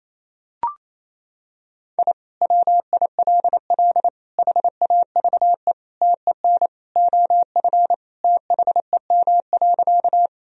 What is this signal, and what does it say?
Morse code, heard as a keyed tone, about 28 wpm, 700 Hz. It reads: I WILL HAVE TEN OF THEM.
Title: Sentences from Top 1000 Words - Review Part 2 - 28wpm